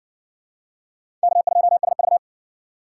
Morse code, heard as a keyed tone, about 60 wpm, 700 Hz. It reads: K2SV